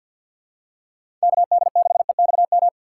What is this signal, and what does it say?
Morse code, heard as a keyed tone, about 46 wpm, 700 Hz. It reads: KD6EXM
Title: USA Call Signs - 6 Characters - 4x Character Spacing - 50wpm